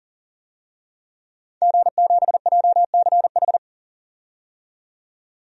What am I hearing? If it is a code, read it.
Morse code, heard as a keyed tone, about 40 wpm, 700 Hz. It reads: G7JCH
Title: DXCC-100 Call Signs - 5 Characters - 40wpm